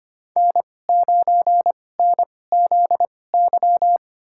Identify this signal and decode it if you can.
Morse code, heard as a keyed tone, about 25 wpm, spaced as for 18 wpm, 700 Hz. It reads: N9NZY
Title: USA Call Signs - 5 Characters - 2x Character Spacing - 25wpm